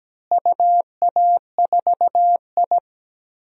Morse code, heard as a keyed tone, about 17 wpm, 700 Hz. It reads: UA4I
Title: DXCC-100 Call Signs - 4 Characters - Repeated 3x - 17wpm